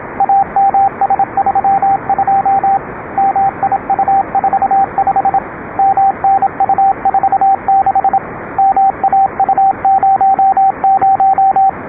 Morse code, heard as a keyed tone, about 26 wpm, 780 Hz, with radio noise. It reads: AMS32 MIU45 MNU46 MAU00